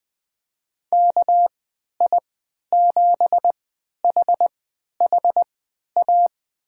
Morse code, heard as a keyed tone, about 20 wpm, spaced as for 11 wpm, 700 Hz. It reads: KI7HHA
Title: USA Call Signs - 6 Characters - 3x Character Spacing - 20wpm